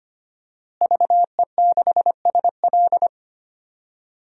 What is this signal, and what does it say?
Morse code, heard as a keyed tone, about 25 wpm, 700 Hz. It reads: VE6SL
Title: DXCC-100 Call Signs - 5 Characters - Repeated 2x - 25wpm